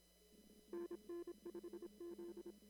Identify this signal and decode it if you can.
Morse code, heard as a keyed tone, about 26 wpm, 340 Hz. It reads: NN5Z
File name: CW / kt5j wpx cw-op-k5pi